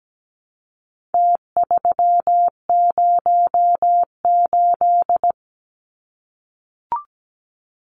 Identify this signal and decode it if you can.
Morse code, heard as a keyed tone, about 17 wpm, 700 Hz. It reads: T308
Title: Numbers - Contest Style - 4-Digit 17wpm